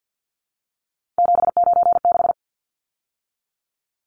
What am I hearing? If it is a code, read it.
Morse code, heard as a keyed tone, about 50 wpm, 700 Hz. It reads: 796
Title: Numbers - 3-Digit 50wpm